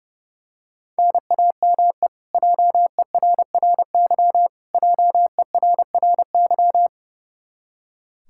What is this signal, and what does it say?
Morse code, heard as a keyed tone, about 30 wpm, 700 Hz. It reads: NAME JERRY JERRY
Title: Practice QSO - Exchange 1 - Beginning - 30wpm